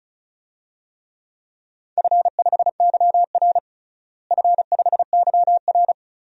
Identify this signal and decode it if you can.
Morse code, heard as a keyed tone, about 35 wpm, 700 Hz. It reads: F5YR F5YR
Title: DXCC-100 Call Signs - 4 Characters - Repeated 2x - 35wpm